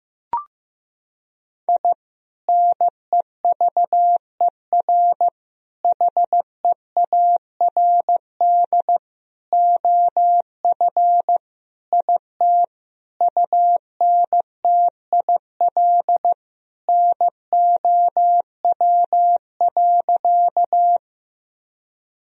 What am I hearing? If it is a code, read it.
Morse code, heard as a keyed tone, about 15 wpm, 700 Hz. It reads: I NEVER HEARD OF IT UNTIL NOW.
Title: Sentences from Top 1000 Words - Review Part 1 - 15wpm